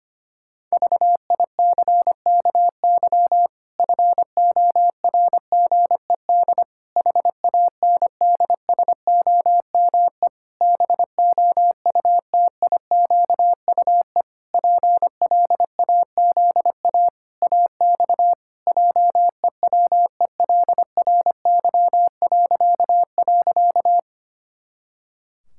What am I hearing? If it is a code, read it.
Morse code, heard as a keyed tone, about 25 wpm, 700 Hz. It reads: VICKY FORGED HANDSOME BOUTIQUE PLAZA AX JEWELRY..
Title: Pangram Challenge v2 - 25wpm